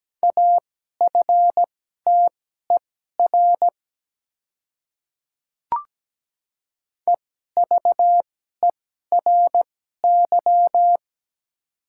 Morse code, heard as a keyed tone, about 17 wpm, spaced as for 12 wpm, 700 Hz. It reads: AFTER EVERY
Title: 2x Character Spacing - Top 300 Words - 17wpm